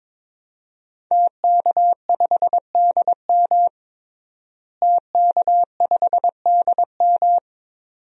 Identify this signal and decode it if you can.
Morse code, heard as a keyed tone, about 22 wpm, 700 Hz. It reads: TK5DM TK5DM
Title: DXCC-100 Call Signs - 5 Characters - Repeated 2x - 22wpm